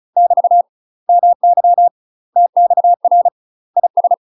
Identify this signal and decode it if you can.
Morse code, heard as a keyed tone, about 34 wpm, 700 Hz. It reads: <BT> MY TXR IS
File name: UK MASTER QSOs TRAINING / 35WPM / UK MASTER QSOs 0004